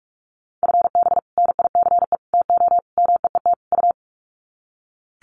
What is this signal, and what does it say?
Morse code, heard as a keyed tone, about 44 wpm, 700 Hz. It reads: FB NICE TO MEET U
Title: Common QSO Phrases 45wpm